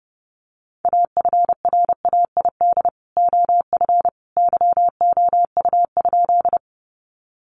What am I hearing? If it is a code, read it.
Morse code, heard as a keyed tone, about 30 wpm, 700 Hz. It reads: AFRAID OF YOU?
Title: Sentences from Top 1500 Words 30wpm